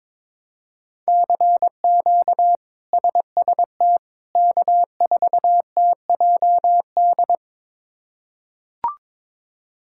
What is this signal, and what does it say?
Morse code, heard as a keyed tone, about 22 wpm, 700 Hz. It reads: CQ SST K4TJD
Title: K1USN SST QSO Practice - 22wpm